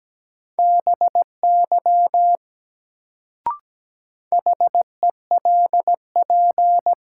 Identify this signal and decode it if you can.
Morse code, heard as a keyed tone, about 17 wpm, 700 Hz. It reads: BY HELP
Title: Sets of 2 Words - New Word Drill - 17wpm